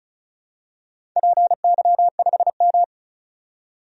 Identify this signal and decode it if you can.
Morse code, heard as a keyed tone, about 36 wpm, 700 Hz. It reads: PY5M
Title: DXCC-100 Call Signs - 4 Characters - 35wpm